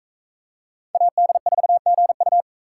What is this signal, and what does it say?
Morse code, heard as a keyed tone, about 39 wpm, 700 Hz. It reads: AD4GU